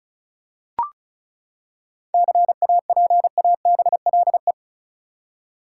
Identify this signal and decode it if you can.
Morse code, heard as a keyed tone, about 34 wpm, 700 Hz. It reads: CAPABLE
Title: Seven Letter Words - 35wpm